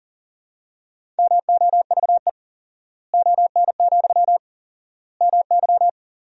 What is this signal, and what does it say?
Morse code, heard as a keyed tone, about 40 wpm, 700 Hz. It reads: MOVE ON, MY
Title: 4x Word-Spacing - Sentences from Top 500 Words - 40wpm